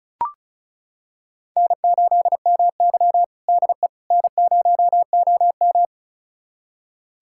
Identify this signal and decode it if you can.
Morse code, heard as a keyed tone, about 34 wpm, 700 Hz. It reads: N8MY DE N0OM
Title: USA Callsign DE Callsign - 4 Characters - 35wpm